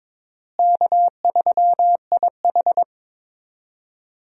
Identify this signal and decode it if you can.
Morse code, heard as a keyed tone, about 22 wpm, 700 Hz. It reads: K3IH
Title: USA Call Signs - 4 Characters - Repeated 3x - 22wpm